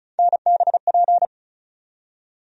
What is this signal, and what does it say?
Morse code, heard as a keyed tone, about 36 wpm, 700 Hz. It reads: NBP
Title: USA Call Sign Postfixes - 3 Letters 35wpm